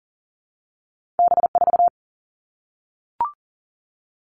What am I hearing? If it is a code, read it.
Morse code, heard as a keyed tone, about 39 wpm, 700 Hz. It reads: B 4
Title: Two Challenging Letter-Number Combinations 40wpm